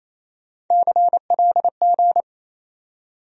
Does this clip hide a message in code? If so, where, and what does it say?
Yes: Morse code from 0.7 to 2.2 s, heard as a keyed tone, about 36 wpm, 700 Hz. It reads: CLG